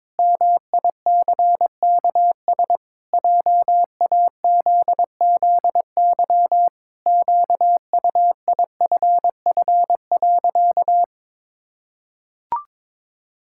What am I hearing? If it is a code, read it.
Morse code, heard as a keyed tone, about 22 wpm, 700 Hz. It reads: MICKS JAZZY QUIFF.
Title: Pangram Challenge v2 - 22wpm